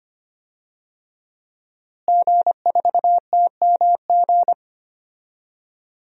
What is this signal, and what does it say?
Morse code, heard as a keyed tone, about 25 wpm, 700 Hz. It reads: G4TMG
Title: DXCC-100 Call Signs - 5 Characters - Repeated 2x - 25wpm